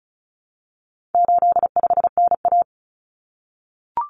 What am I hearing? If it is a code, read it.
Morse code, heard as a keyed tone, about 34 wpm, 700 Hz. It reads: 85NA
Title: Common Cut Numbers - 4-Digit 35wpm